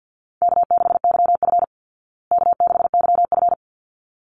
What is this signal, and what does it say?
Morse code, heard as a keyed tone, about 48 wpm, 700 Hz. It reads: K6YF K6YF
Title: Easy Call Signs Repeated 2x - 50wpm